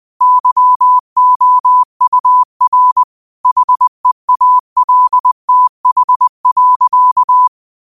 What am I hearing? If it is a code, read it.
Morse code, heard as a keyed tone, about 20 wpm, 1000 Hz. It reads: YOUR HEALTH.
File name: qotdfiles / 20wpm